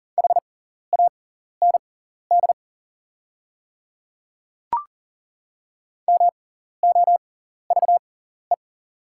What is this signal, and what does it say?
Morse code, heard as a keyed tone, about 38 wpm, spaced as for 14 wpm, 690 Hz. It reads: HAND MOVE
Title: 6x Character Spacing - Top 200 Words - 40wpm